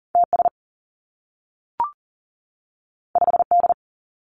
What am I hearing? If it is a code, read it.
Morse code, heard as a keyed tone, about 40 wpm, 700 Hz. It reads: TS 5D
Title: Two Challenging Letter-Number Combinations 40wpm